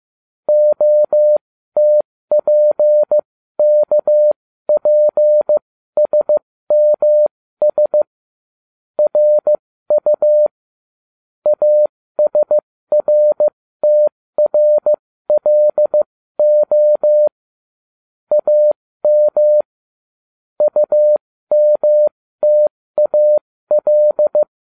Morse code, heard as a keyed tone, about 15 wpm, spaced as for 13 wpm, 600 Hz. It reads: OTPKPSMS RU ASRTRLO AM UMTAL